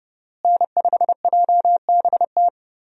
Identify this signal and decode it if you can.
Morse code, heard as a keyed tone, about 31 wpm, 700 Hz. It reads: N5JBT
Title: USA Call Signs - 5 Characters - Repeated 3x - 30wpm